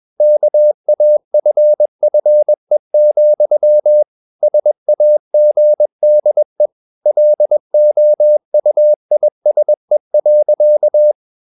Morse code, heard as a keyed tone, about 21 wpm, 600 Hz. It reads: KAFFE, SAGDE LOUISE.
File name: Ferie QRS 21wpm